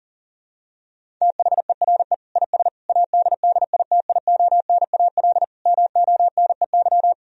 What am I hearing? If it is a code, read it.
Morse code, heard as a keyed tone, about 40 wpm, 700 Hz. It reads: THERE IS ADDITIONAL MONEY
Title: Sentences from Top 1000 Words - Review Part 4 - 40wpm